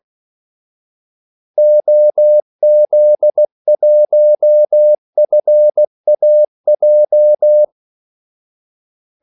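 Morse code, heard as a keyed tone, about 16 wpm, 600 Hz. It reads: OZ1FAJ
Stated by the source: Morse call sign practice at 16 wpm.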